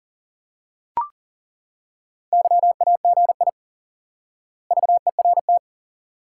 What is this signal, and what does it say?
Morse code, heard as a keyed tone, about 39 wpm, 700 Hz. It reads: YAGI VERT
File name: IQE Course 3 - Lesson 89 - YAGI - Two Elements - 40wpm